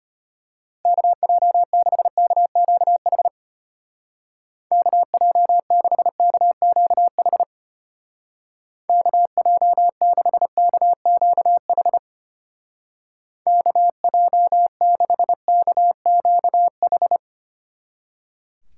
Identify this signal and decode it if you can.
Morse code, heard as a keyed tone, about 32 wpm, 700 Hz. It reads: KJ6KQH KJ6KQH KJ6KQH KJ6KQH